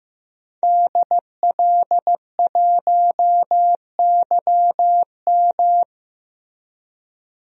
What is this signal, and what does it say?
Morse code, heard as a keyed tone, about 15 wpm, 700 Hz. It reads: DL1YM
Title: DXCC-100 Call Signs - 5 Characters - Repeated 2x - 15wpm